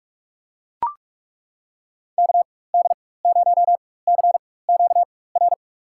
Morse code, heard as a keyed tone, about 42 wpm, spaced as for 21 wpm, 700 Hz. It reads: KD0CQR
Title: USA Call Signs - 6 Characters - 4x Character Spacing - 45wpm